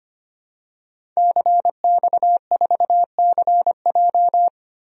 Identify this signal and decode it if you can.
Morse code, heard as a keyed tone, about 25 wpm, 700 Hz. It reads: CX4CJ
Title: DXCC-100 Call Signs - 5 Characters - Repeated 3x - 25wpm